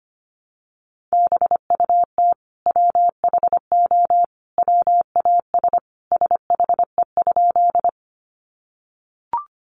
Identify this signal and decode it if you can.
Morse code, heard as a keyed tone, about 25 wpm, 700 Hz. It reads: BUT WHO WAS SHE?
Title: Sentences from Top 100 Words 25wpm